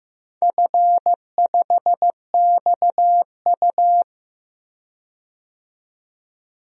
Morse code, heard as a keyed tone, about 15 wpm, 700 Hz. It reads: F5XU
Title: DXCC-100 Call Signs - 4 Characters - 15wpm